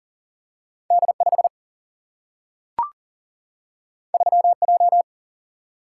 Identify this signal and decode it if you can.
Morse code, heard as a keyed tone, about 40 wpm, 700 Hz. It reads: D5 3J
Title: Single-Letter Single-Number 40wpm